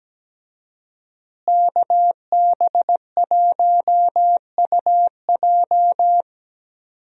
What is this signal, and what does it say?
Morse code, heard as a keyed tone, about 17 wpm, 700 Hz. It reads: KB1UJ
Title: USA Call Signs - 5 Characters - Repeated 3x - 17wpm